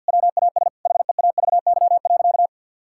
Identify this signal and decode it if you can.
Morse code, heard as a keyed tone, about 50 wpm, 700 Hz. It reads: WAS HEAVY.
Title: Sentences from Top 1600 Words 50wpm